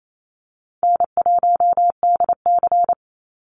Morse code, heard as a keyed tone, about 28 wpm, 700 Hz. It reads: N1DC
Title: Easy Call Signs - 28wpm